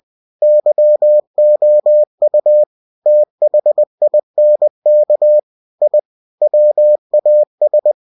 Morse code, heard as a keyed tone, about 20 wpm, 600 Hz. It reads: YOU THINK I WAS
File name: cw-books / winnie-the-pooh / winnie-20wpm-page-0087